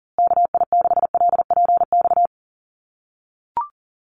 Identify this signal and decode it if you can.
Morse code, heard as a keyed tone, about 40 wpm, 700 Hz. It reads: KI6LPX
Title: USA Call Signs Repeated 3x 40wpm